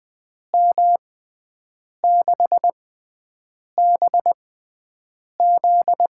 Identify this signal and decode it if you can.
Morse code, heard as a keyed tone, about 20 wpm, spaced as for 7 wpm, 700 Hz. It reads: M6BZ
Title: USA Call Signs - 6 Characters - 6x Character Spacing - 20wpm